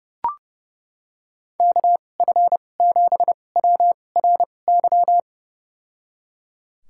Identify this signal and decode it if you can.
Morse code, heard as a keyed tone, about 31 wpm, spaced as for 22 wpm, 700 Hz. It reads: KF7WRY
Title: USA Call Signs - 2x Character Spacing - 30wpm